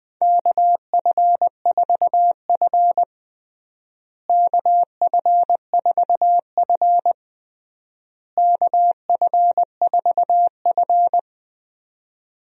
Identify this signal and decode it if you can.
Morse code, heard as a keyed tone, about 20 wpm, 700 Hz. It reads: KF4F KF4F KF4F
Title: USA Call Signs - 4 Characters - Repeated 3x - 20wpm